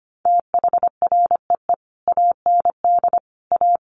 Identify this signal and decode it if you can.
Morse code, heard as a keyed tone, about 25 wpm, 700 Hz. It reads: THREE AND A
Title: Sets of 5 Words Encore2 25wpm